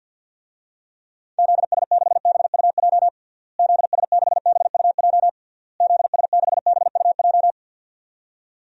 Morse code, heard as a keyed tone, about 52 wpm, 700 Hz. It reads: ZS6BUJ ZS6BUJ ZS6BUJ
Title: DXCC-100 Call Signs - 6 Characters - Repeated 3x - 50wpm